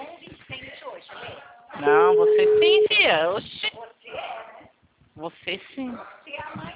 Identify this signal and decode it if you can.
Ringback tone (Europe, 425 Hz), with speech.
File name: out- -2013- - - .173